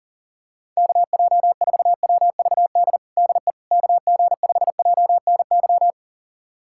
Morse code, heard as a keyed tone, about 40 wpm, 700 Hz. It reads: KJ4WVD DE KG5JNY
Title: USA Callsign DE Callsign - 6 Characters - 40wpm